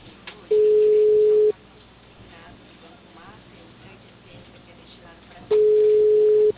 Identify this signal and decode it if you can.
Ringback tone (Europe, 425 Hz), with speech.
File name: out- -2046- - - .32055